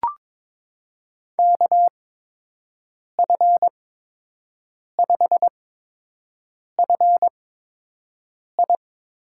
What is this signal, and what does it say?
Morse code, heard as a keyed tone, about 22 wpm, spaced as for 6 wpm, 700 Hz. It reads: KF5FI
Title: USA Call Signs - 8x Character Spacing - 22wpm